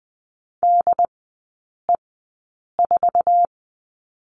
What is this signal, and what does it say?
Morse code, heard as a keyed tone, about 19 wpm, spaced as for 8 wpm, 700 Hz. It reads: DE4
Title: 074 Lesson 13e - Letter L - Three Characters 20wpm